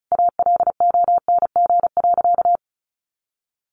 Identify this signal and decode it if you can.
Morse code, heard as a keyed tone, about 34 wpm, 700 Hz. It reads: ALONG.
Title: Sentences from Top 700 Words 35wpm